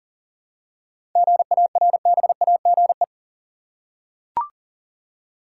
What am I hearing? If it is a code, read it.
Morse code, heard as a keyed tone, about 40 wpm, 700 Hz. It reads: GARBAGE
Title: Seven Letter Words - 40wpm